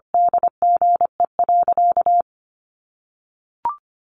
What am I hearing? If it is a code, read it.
Morse code, heard as a keyed tone, about 25 wpm, 700 Hz. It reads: DGE.
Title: Sentences from Top 600 Words 25wpm